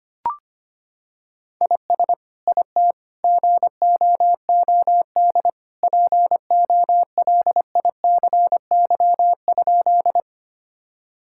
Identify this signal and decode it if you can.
Morse code, heard as a keyed tone, about 25 wpm, 700 Hz. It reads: IS IT GOOD POLICY?
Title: Sentences from Top 1000 Words - Review Part 5 - 25wpm